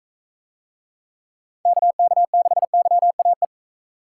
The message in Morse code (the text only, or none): KK6YAE